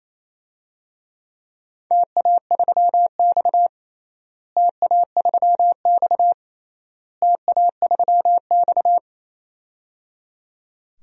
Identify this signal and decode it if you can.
Morse code, heard as a keyed tone, about 28 wpm, 700 Hz. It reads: TA3X TA3X TA3X
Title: DXCC-100 Call Signs - 4 Characters - Repeated 3x - 28wpm